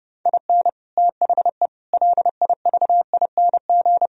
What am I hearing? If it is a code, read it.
Morse code, heard as a keyed tone, about 36 wpm, 700 Hz. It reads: IN THE LIVING
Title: Speed-Racing - Sets of 4 Words 30wpm